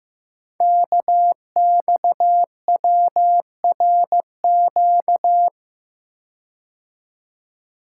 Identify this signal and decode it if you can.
Morse code, heard as a keyed tone, about 15 wpm, 700 Hz. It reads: KXWRQ
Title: Code Groups - 5 Letters - 15wpm